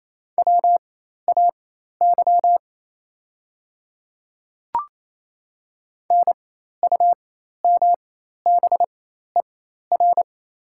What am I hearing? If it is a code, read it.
Morse code, heard as a keyed tone, about 27 wpm, spaced as for 13 wpm, 700 Hz. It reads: WAY NUMBER
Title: 4x Character Spacing - Top 100 Words - 28wpm